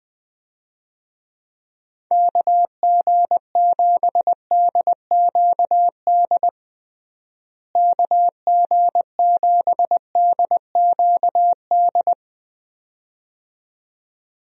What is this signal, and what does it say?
Morse code, heard as a keyed tone, about 20 wpm, 700 Hz. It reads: KG7DQD KG7DQD
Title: USA Call Signs - 6 Characters - Repeated 2x - 20wpm